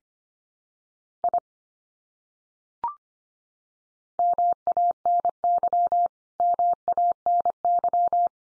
Morse code, heard as a keyed tone, about 25 wpm, 700 Hz. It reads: I MANY MANY
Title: Top 100 Words 25wpm x3